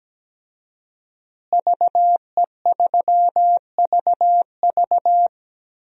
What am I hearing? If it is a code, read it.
Morse code, heard as a keyed tone, about 17 wpm, 700 Hz. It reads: VE3VV
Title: DXCC-100 Call Signs - 5 Characters - Repeated 3x - 17wpm